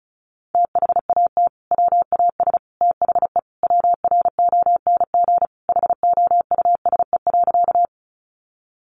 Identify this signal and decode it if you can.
Morse code, heard as a keyed tone, about 34 wpm, 700 Hz. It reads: THAT WAS THE WRONG HOUSE.
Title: Sentences from Top 900 Words 35wpm